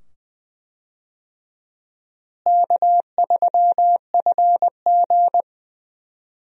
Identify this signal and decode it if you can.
Morse code, heard as a keyed tone, about 21 wpm, 700 Hz. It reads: K3FG
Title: USA Call Signs - 4 Characters - Repeated 2x - 20wpm